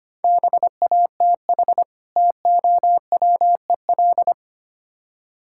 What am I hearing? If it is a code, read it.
Morse code, heard as a keyed tone, about 26 wpm, 700 Hz. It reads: BATH TOWEL
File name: Easy Words - Speed-Racing with Hints - 25wpm